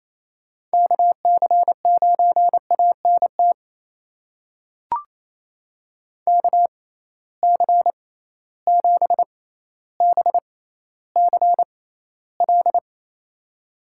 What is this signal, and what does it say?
Morse code, heard as a keyed tone, about 28 wpm, spaced as for 14 wpm, 700 Hz. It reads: KC9ANT KC7BCL
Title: USA Call Signs - 6 Characters - 6x Character Spacing - 28wpm